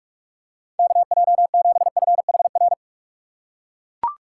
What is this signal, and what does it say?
Morse code, heard as a keyed tone, about 44 wpm, 700 Hz. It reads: KJ7FHR
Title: USA Call Signs - 6x Character Spacing - 45wpm